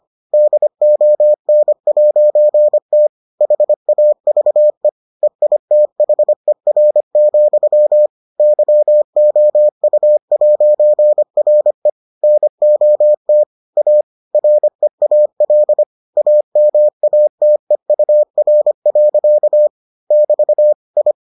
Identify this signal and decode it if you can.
Morse code, heard as a keyed tone, about 25 wpm, 600 Hz. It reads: DON'T HAVE EITHER, YOU'RE NOT A REAL AMATEUR. <BT> I